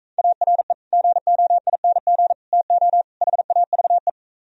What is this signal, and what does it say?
Morse code, heard as a keyed tone, about 43 wpm, 700 Hz. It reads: ARE GOING TO HAVE